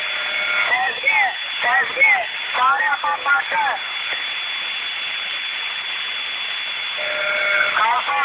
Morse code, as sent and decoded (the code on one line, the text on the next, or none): none